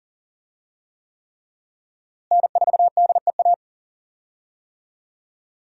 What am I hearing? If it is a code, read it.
Morse code, heard as a keyed tone, about 40 wpm, 700 Hz. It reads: N4DEA
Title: USA Call Signs - 5 Characters - Repeated 3x - 40wpm